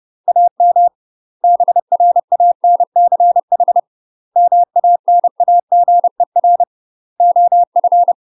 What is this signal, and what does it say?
Morse code, heard as a keyed tone, about 30 wpm, 700 Hz. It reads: AM BRANCH MANAGER OF
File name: UK MASTER QSOs TRAINING / 30WPM / UK MASTER QSOs 0018